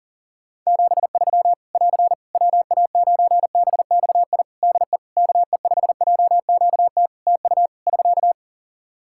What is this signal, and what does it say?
Morse code, heard as a keyed tone, about 40 wpm, 700 Hz. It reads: 73 <AR> WA9BXI DE KE5JQT TU <SK>
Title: Practice QSO - Exchange 4 - Ending - 40wpm